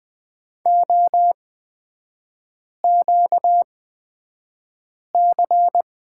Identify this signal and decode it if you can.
Morse code, heard as a keyed tone, about 20 wpm, spaced as for 5 wpm, 700 Hz. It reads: OQC
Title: 250 Lesson 40e - BK - Three Characters 20wpm